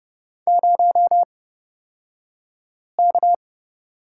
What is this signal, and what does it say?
Morse code, heard as a keyed tone, about 30 wpm, spaced as for 5 wpm, 700 Hz. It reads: 0K
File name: Lesson 37d - Letter x - Two Characters 30wpm